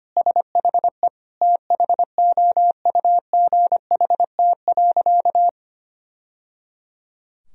Morse code, heard as a keyed tone, about 25 wpm, 700 Hz. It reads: SHE THOUGHT.
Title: Sentences from Top 1000 Words - Review Part 3 - 25wpm